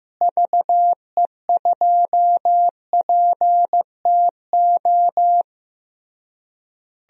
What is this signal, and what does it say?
Morse code, heard as a keyed tone, about 15 wpm, 700 Hz. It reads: VE2PTO
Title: DXCC-100 Call Signs - 6 Characters - Repeated 3x - 15wpm